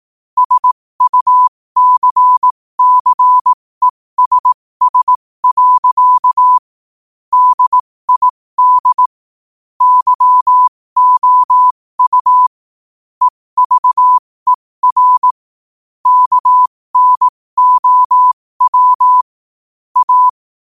Morse code, heard as a keyed tone, about 18 wpm, spaced as for 15 wpm, 1000 Hz. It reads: SUCCESS. DID YOU EVER KNOW A
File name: qotdfiles / 15wpm